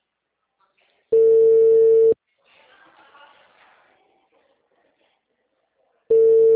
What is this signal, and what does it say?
Ringback tone (China, 450 Hz).
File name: out- -2002- - - .1650